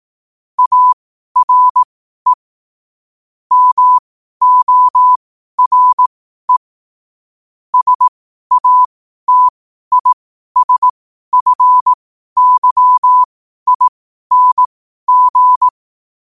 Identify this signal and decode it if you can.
Morse code, heard as a keyed tone, about 18 wpm, spaced as for 12 wpm, 1000 Hz. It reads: ARE MORE SATISFYING
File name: qotdfiles / 12wpm